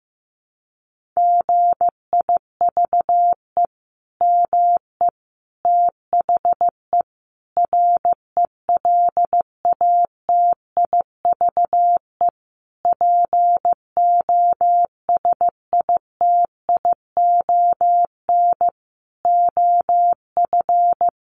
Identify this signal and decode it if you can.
Morse code, heard as a keyed tone, about 15 wpm, 700 Hz. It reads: GIVE ME THE RELATIVE POSITION OF
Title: Sentences from Top 1400 Words 15wpm